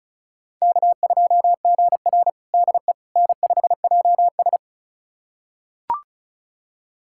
Morse code, heard as a keyed tone, about 36 wpm, 700 Hz. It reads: K2GR DE N5JS
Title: USA Callsign DE Callsign - 4 Characters - 35wpm